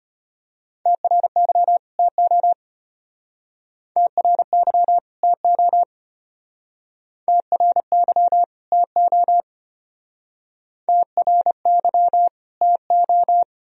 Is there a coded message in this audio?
Yes: Morse code from 0.9 to 13.4 s, heard as a keyed tone, about 33 wpm, 700 Hz. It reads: TRY TO TRY TO TRY TO TRY TO